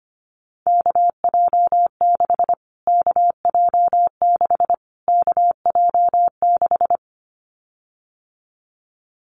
Morse code, heard as a keyed tone, about 25 wpm, 700 Hz. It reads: KJ6 KJ6 KJ6
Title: Callsign Prefixes for Top 25 DXCC - Repeated 3x - 25wpm